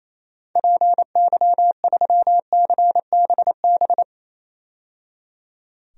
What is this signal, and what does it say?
Morse code, heard as a keyed tone, about 28 wpm, 700 Hz. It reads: PY3CBB
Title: DXCC-100 Call Signs - 6 Characters - 28wpm